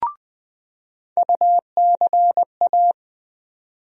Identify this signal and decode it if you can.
Morse code, heard as a keyed tone, about 21 wpm, 700 Hz. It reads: UCA
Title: USA Call Sign Postfixes - 3 Letters 20wpm